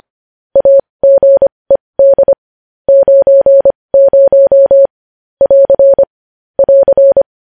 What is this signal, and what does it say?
Morse code, heard as a keyed tone, about 25 wpm, spaced as for 20 wpm, 550 Hz. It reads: AGED 90 <AR> <AR>
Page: news-headlines-in-morse-code-at-20-wpm